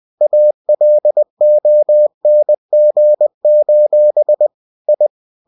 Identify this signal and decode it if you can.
Morse code, heard as a keyed tone, about 20 wpm, 600 Hz. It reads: ALONG: I
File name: cw-books / winnie-the-pooh / winnie-20wpm-page-0005